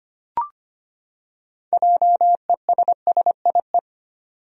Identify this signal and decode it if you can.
Morse code, heard as a keyed tone, about 25 wpm, 700 Hz. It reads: JESSIE
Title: USA Common Names - 25wpm